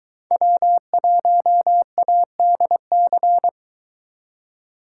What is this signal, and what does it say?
Morse code, heard as a keyed tone, about 22 wpm, 700 Hz. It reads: W1ADC